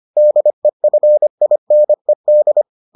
Morse code, heard as a keyed tone, about 25 wpm, 600 Hz. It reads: DEFINED